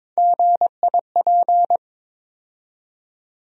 Morse code, heard as a keyed tone, about 22 wpm, 700 Hz. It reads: GIP
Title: USA Call Sign Postfixes - 3 Letters 22wpm